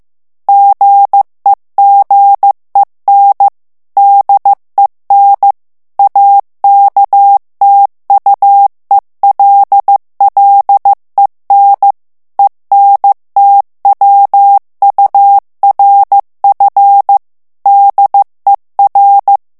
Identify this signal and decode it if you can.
Morse code, heard as a keyed tone, about 15 wpm, 800 Hz. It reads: GEGEN DEN AKTUELLEN ENTWURF DER
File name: ALLEMAND texte clair 1 vitesse 14 mots minute